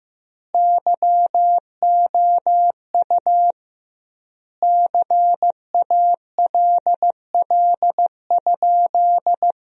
Morse code, heard as a keyed tone, about 15 wpm, 700 Hz. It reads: YOU CALL?
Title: 2x Word-Spacing - Sentences from Top 100 Words - 15wpm